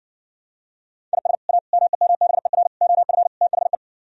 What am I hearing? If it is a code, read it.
Morse code, heard as a keyed tone, about 60 wpm, 700 Hz. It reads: IS A MEMBER OF THE